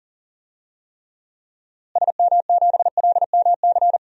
Morse code, heard as a keyed tone, about 40 wpm, 700 Hz. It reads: SM7LMC
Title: DXCC-100 Call Signs - 6 Characters - Repeated 2x - 40wpm